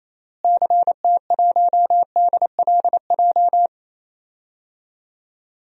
Morse code, heard as a keyed tone, about 28 wpm, 700 Hz. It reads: CT1DLJ
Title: DXCC-100 Call Signs - 6 Characters - Repeated 3x - 28wpm